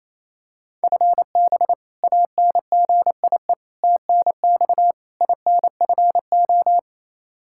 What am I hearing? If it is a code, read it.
Morse code, heard as a keyed tone, about 28 wpm, 700 Hz. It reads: FB ANGIE TNX INFO
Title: K1USN SST QSO Practice 28wpm